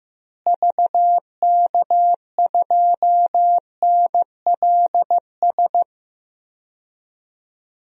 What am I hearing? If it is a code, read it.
Morse code, heard as a keyed tone, about 15 wpm, 700 Hz. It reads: VK2NLS
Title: DXCC-100 Call Signs - 6 Characters - Repeated 3x - 15wpm